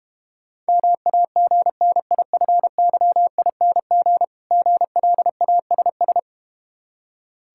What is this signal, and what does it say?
Morse code, heard as a keyed tone, about 32 wpm, 700 Hz. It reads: MAGNIFYING GLASS